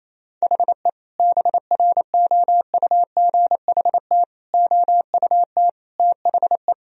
Morse code, heard as a keyed tone, about 27 wpm, 700 Hz. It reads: HE BROUGHT OUT THE
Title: Sentences from Top 1000 Words - Review Part 3 - 28wpm